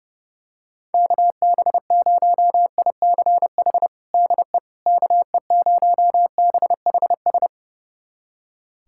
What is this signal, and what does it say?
Morse code, heard as a keyed tone, about 30 wpm, 700 Hz. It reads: KB0ICH DE KE0BHS
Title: USA Callsign DE Callsign - 6 Characters - 30wpm